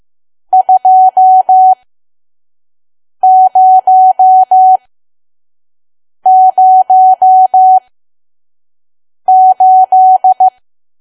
Morse code, heard as a keyed tone, about 15 wpm, spaced as for 5 wpm, 750 Hz. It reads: 2008
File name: Archive / 5 WPM files / PM